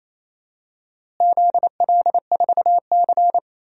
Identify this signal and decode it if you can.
Morse code, heard as a keyed tone, about 28 wpm, 700 Hz. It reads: ZL4C
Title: DXCC-100 Call Signs - 4 Characters - Repeated 2x - 28wpm